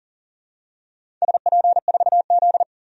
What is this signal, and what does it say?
Morse code, heard as a keyed tone, about 41 wpm, 700 Hz. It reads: SP4Z
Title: DXCC-100 Call Signs - 4 Characters - 40wpm